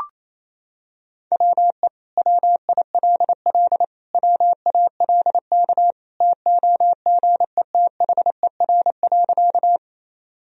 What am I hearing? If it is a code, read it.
Morse code, heard as a keyed tone, about 27 wpm, 700 Hz. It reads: WE WILL WALK TOGETHER.
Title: Sentences from Top 1000 Words - Review Part 1 - 28wpm